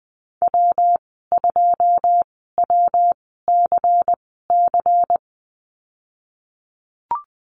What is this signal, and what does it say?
Morse code, heard as a keyed tone, about 20 wpm, spaced as for 15 wpm, 700 Hz. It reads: W2WCC
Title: 186 Lesson 30g - Letter b - Call Signs 20wpm